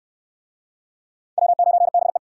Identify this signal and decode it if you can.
Morse code, heard as a keyed tone, about 63 wpm, 700 Hz. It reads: W9BE